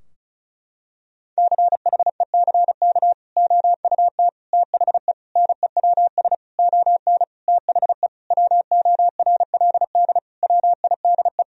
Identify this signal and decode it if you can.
Morse code, heard as a keyed tone, about 36 wpm, 700 Hz. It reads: CHECK OUT THE NEWS ON THE WORLD WIDE
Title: Sentences from Top 1000 Words - Review Part 4 - 35wpm